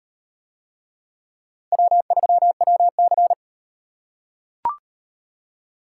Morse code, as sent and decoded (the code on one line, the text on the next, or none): .-- ...-- .-- -.-.
W3WC